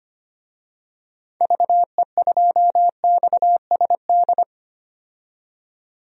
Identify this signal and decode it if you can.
Morse code, heard as a keyed tone, about 25 wpm, 700 Hz. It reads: VE2XSD
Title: DXCC-100 Call Signs - 6 Characters - 25wpm